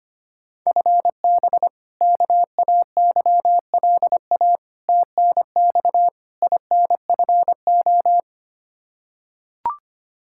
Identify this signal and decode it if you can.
Morse code, heard as a keyed tone, about 25 wpm, 700 Hz. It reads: FB KAYLA TNX INFO
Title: K1USN SST QSO Practice 25wpm